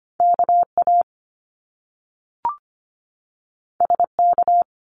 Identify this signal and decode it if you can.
Morse code, heard as a keyed tone, about 25 wpm, 700 Hz. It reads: KA SK
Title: Top Two Letter Combinations in Words 25wpm